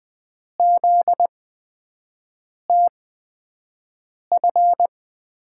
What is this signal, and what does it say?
Morse code, heard as a keyed tone, about 20 wpm, spaced as for 5 wpm, 700 Hz. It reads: ZTF
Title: USA Call Signs - 5 Characters - 8x Character Spacing - 20wpm